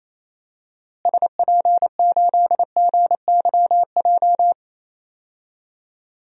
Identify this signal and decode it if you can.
Morse code, heard as a keyed tone, about 28 wpm, 700 Hz. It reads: SP8GYJ
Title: DXCC-100 Call Signs - 6 Characters - Repeated 2x - 28wpm